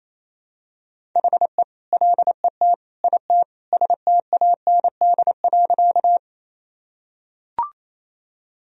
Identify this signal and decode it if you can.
Morse code, heard as a keyed tone, about 28 wpm, 700 Hz. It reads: HE LET IT STAND.
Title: Sentences from Top 1000 Words - Review Part 1 - 28wpm